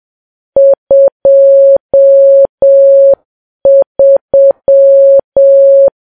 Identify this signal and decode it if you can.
Morse code, heard as a keyed tone, about 7 wpm, 550 Hz. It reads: 23
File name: Files Web tls / Morse on9fv / morse7kps